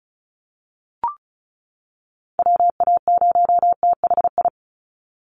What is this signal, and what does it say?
Morse code, heard as a keyed tone, about 34 wpm, 700 Hz. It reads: WA0THI
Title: USA Call Signs 35wpm